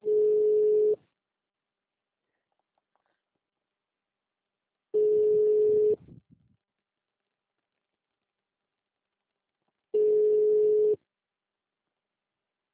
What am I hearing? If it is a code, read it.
Ringback tone (Europe, 425 Hz).